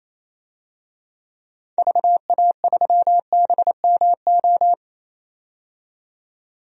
Morse code, heard as a keyed tone, about 28 wpm, 700 Hz. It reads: VA3BMO
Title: DXCC-100 Call Signs - 6 Characters - 28wpm